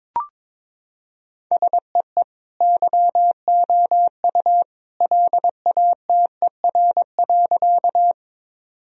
Morse code, heard as a keyed tone, about 22 wpm, 700 Hz. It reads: SEE YOU LATER.
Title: Sentences from Top 1000 Words - Review Part 4 - 22wpm